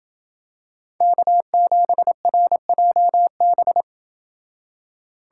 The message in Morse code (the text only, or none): K7RJB